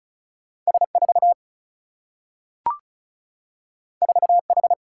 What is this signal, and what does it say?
Morse code, heard as a keyed tone, about 36 wpm, 700 Hz. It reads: S 4 4 H
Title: Single-Letter Single-Number 35wpm